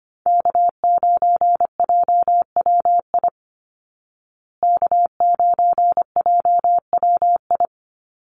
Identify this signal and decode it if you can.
Morse code, heard as a keyed tone, about 25 wpm, 700 Hz. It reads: K9JWI K9JWI
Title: Call Signs Repeated 2x - 25wpm